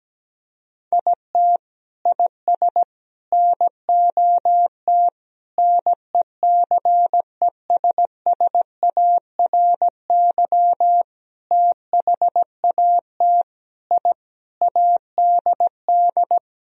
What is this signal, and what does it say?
Morse code, heard as a keyed tone, about 17 wpm, 700 Hz. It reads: IT IS NOT NECESSARY THAT I ADD
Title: Sentences from Top 1000 Words - Review Part 1 - 17wpm